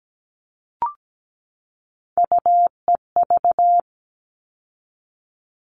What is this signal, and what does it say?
Morse code, heard as a keyed tone, about 17 wpm, 700 Hz. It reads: UEV
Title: All Cut Numbers - 3-Digit 17wpm